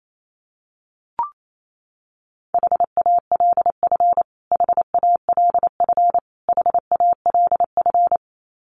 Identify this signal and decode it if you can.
Morse code, heard as a keyed tone, about 28 wpm, 700 Hz. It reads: HALF HALF HALF
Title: Top 500 Words Repeated 28wpm